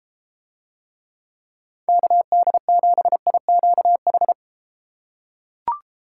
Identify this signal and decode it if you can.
Morse code, heard as a keyed tone, about 32 wpm, 700 Hz. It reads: KD7IQH